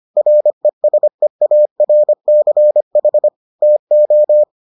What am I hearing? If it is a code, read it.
Morse code, heard as a keyed tone, about 25 wpm, 600 Hz. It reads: RESEARCH TO